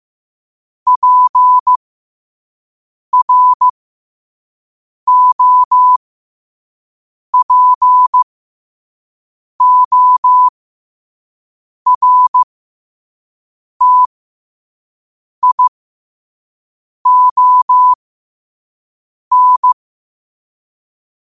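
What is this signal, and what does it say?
Morse code, heard as a keyed tone, about 15 wpm, spaced as for 5 wpm, 1000 Hz. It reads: PROPORTION
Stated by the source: Morse code quote of the day at 5 wpm.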